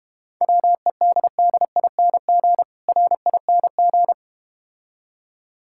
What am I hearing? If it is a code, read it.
Morse code, heard as a keyed tone, about 32 wpm, 700 Hz. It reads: WEDDING RING